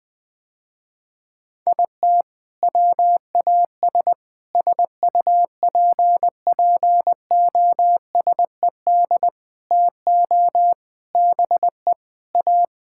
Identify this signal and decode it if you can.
Morse code, heard as a keyed tone, about 19 wpm, 700 Hz. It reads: IT WAS SUPPOSED TO BE A
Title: Sentences with 4-Word Phrases - Spoken Beginning - 20wpm